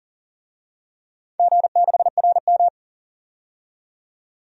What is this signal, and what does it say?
Morse code, heard as a keyed tone, about 40 wpm, 700 Hz. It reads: G6RM
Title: DXCC-100 Call Signs - 4 Characters - 40wpm